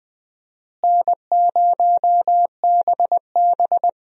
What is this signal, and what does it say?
Morse code, heard as a keyed tone, about 20 wpm, 700 Hz. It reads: N0BB
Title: USA Call Signs - 4 Characters - Repeated 3x - 20wpm